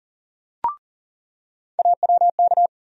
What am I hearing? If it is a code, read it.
Morse code, heard as a keyed tone, about 40 wpm, 700 Hz. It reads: AWK
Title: USA Call Sign Postfixes - 3 Letters 40wpm